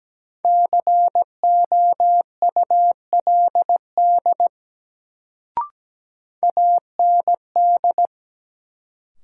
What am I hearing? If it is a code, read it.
Morse code, heard as a keyed tone, about 17 wpm, 700 Hz. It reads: COULD AND
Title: Sentences from Top 100 Words - New Word Drill - 17wpm